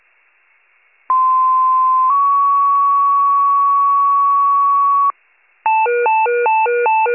Two-tone page, heard as a one-second tone, then a three-second tone, about 1040 then 1120 Hz. It is heard from 1.1 to 5.1 s.
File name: pager / 95 RESCUE / wcalert